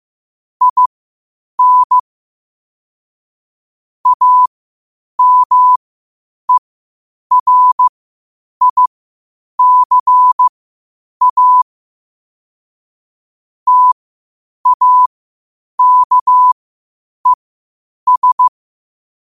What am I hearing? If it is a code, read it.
Morse code, heard as a keyed tone, about 15 wpm, spaced as for 8 wpm, 1000 Hz. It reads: IN AMERICA TAKES
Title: 8 WPM morse code quotes for Fri, 01 Mar 2024